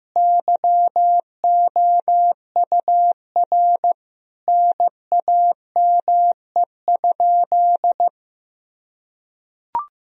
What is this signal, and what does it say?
Morse code, heard as a keyed tone, about 15 wpm, 700 Hz. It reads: YOUR NAME?
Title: Sentences from Top 1000 Words - Review Part 1 - 15wpm